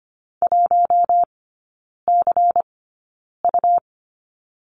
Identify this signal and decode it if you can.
Morse code, heard as a keyed tone, about 25 wpm, spaced as for 9 wpm, 700 Hz. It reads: 1CU
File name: Lesson 17e - Letter u - Three Characters 25wpm